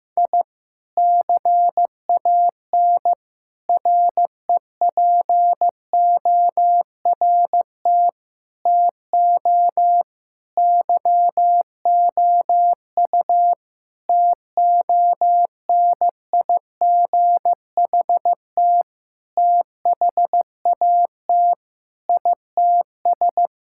Morse code, heard as a keyed tone, about 15 wpm, 700 Hz. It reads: I CAN REPORT TO YOU TONIGHT THAT ITS
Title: Sentences from Top 1000 Words - Review Part 6 - 15wpm